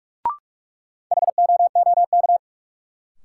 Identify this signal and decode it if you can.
Morse code, heard as a keyed tone, about 46 wpm, 700 Hz. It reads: HOOK
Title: Sentences from Top 1600 Words - New Word Drill Alt - 45wpm